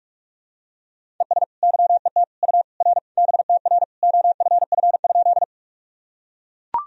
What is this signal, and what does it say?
Morse code, heard as a keyed tone, about 42 wpm, 700 Hz. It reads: ES YET U R BTR OFF?
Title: Sentences from Top 500 Words and Abbreviations 45wpm